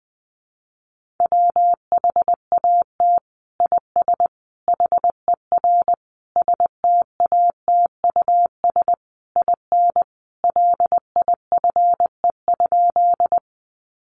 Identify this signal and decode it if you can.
Morse code, heard as a keyed tone, about 20 wpm, 700 Hz. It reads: WHAT IS HER STATUS IN LIFE?
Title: Sentences from Top 1400 Words 20wpm